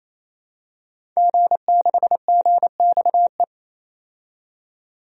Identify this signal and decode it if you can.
Morse code, heard as a keyed tone, about 27 wpm, 700 Hz. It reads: G6GXE
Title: DXCC-100 Call Signs - 5 Characters - 28wpm